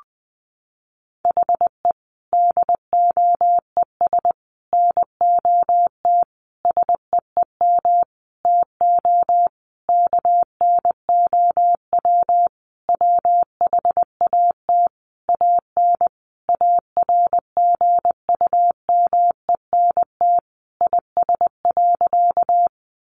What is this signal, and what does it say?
Morse code, heard as a keyed tone, about 20 wpm, 700 Hz. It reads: HE DOES NOT SEEM TO KNOW WHAT AN ARGUMENT IS.
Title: Sentences from Top 1400 Words 20wpm